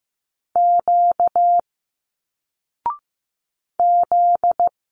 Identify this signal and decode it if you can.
Morse code, heard as a keyed tone, about 15 wpm, 700 Hz. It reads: Q Z
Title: Single Letters 15wpm